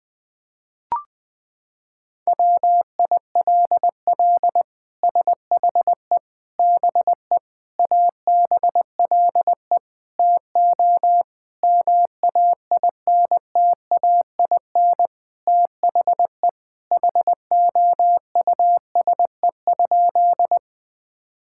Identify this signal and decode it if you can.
Morse code, heard as a keyed tone, about 21 wpm, 700 Hz. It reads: WILL SHE BE ABLE TO MAINTAIN THE HOUSE?
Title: Sentences from Top 1000 Words - Review Part 5 - 20wpm